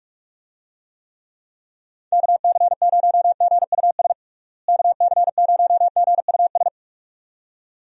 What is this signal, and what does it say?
Morse code, heard as a keyed tone, about 46 wpm, 700 Hz. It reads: KC0GUS KC0GUS
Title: USA Call Signs - 6 Characters - Repeated 2x - 45wpm